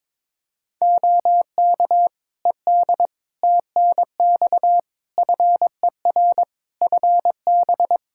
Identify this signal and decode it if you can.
Morse code, heard as a keyed tone, about 21 wpm, 700 Hz. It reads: OK ED TNX FER FB
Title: Practice QSO - Exchange 4 - Ending - 22wpm